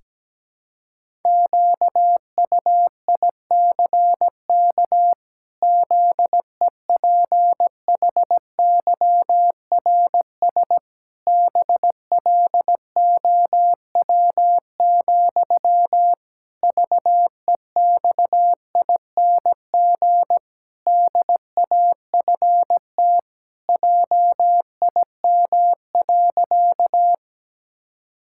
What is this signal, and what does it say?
Morse code, heard as a keyed tone, about 17 wpm, 700 Hz. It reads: QUICK ZEPHYRS BLOW, VEXING DAFT JIM.
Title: Pangram Challenge v2 - 17wpm